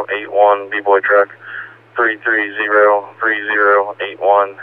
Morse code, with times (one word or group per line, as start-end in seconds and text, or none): none